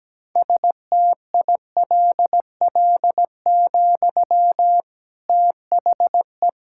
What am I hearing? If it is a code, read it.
Morse code, heard as a keyed tone, about 17 wpm, 700 Hz. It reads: STILL, THE
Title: Sentences from Top 1600 Words 17wpm